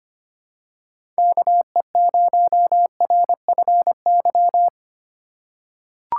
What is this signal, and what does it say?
Morse code, heard as a keyed tone, about 25 wpm, 700 Hz. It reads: KE0RFY
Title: USA Call Signs - 4x Character Spacing - 25wpm